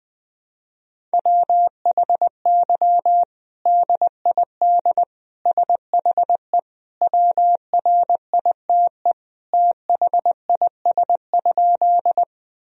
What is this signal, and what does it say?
Morse code, heard as a keyed tone, about 21 wpm, 700 Hz. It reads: WHY DID SHE WRITE THIS?
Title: Sentences from Top 1000 Words - Review Part 1 - 20wpm